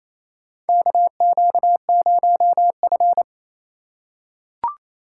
Morse code, heard as a keyed tone, about 28 wpm, 700 Hz. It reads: KQ0F
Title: USA Call Signs - 4 Characters - 4x Character Spacing - 28wpm